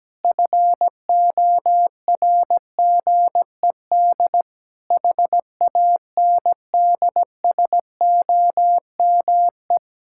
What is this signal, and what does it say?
Morse code, heard as a keyed tone, about 17 wpm, 700 Hz. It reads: FORGED HANDSOME
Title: Pangram Challenge v2 - 17wpm